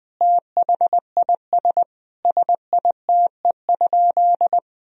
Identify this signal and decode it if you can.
Morse code, heard as a keyed tone, about 20 wpm, 700 Hz. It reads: THIS SITE?
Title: Sentences from Top 1000 Words - Review Part 4 - 20wpm